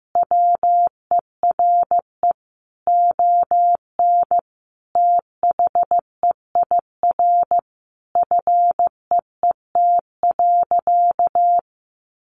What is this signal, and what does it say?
Morse code, heard as a keyed tone, about 15 wpm, 700 Hz. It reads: WERE ON THEIR FEET.
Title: Sentences from Top 1000 Words 15wpm